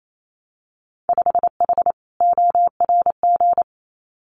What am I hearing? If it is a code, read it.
Morse code, heard as a keyed tone, about 28 wpm, 690 Hz. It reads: 5H ORG
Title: Winter Field Day Exchanges Encore 28wpm